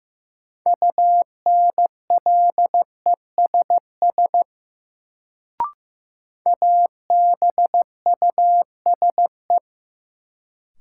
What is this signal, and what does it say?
Morse code, heard as a keyed tone, about 15 wpm, 700 Hz. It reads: UNLESS ABUSE
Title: Sentences from Top 1300 Words - New Word Drill Alt - 15wpm